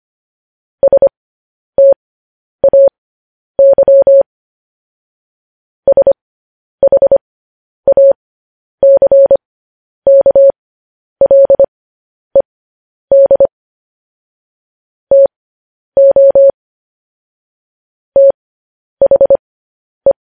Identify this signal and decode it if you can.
Morse code, heard as a keyed tone, about 25 wpm, spaced as for 10 wpm, 550 Hz. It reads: STAY SHACKLED TO THE
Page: news headlines in morse code at 10 wpm